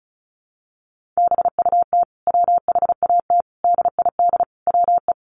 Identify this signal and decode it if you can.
Morse code, heard as a keyed tone, about 34 wpm, 700 Hz. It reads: BUT WHAT DID WE
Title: Sentences from Top 400 Words 35wpm